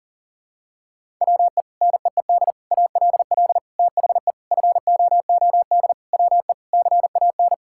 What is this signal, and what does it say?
Morse code, heard as a keyed tone, about 40 wpm, 700 Hz. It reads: WE NEED ALL THE FOOD WE CAN
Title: Sentences from Top 1000 Words - Review Part 5 - 40wpm